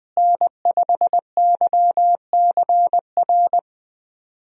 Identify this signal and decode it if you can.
Morse code, heard as a keyed tone, about 20 wpm, 700 Hz. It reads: N5YCR
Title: USA Call Signs - 5 Characters - Repeated 2x - 20wpm